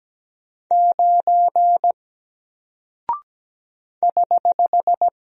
Morse code, heard as a keyed tone, about 17 wpm, 700 Hz. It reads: 9 <HH>
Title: Single Character (Letter:Number:Punctuation:Prosign) 17wpm